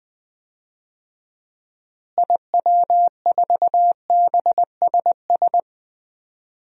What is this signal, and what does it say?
Morse code, heard as a keyed tone, about 20 wpm, 700 Hz. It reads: IW4BSS
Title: DXCC-100 Call Signs - 6 Characters - Repeated 3x - 20wpm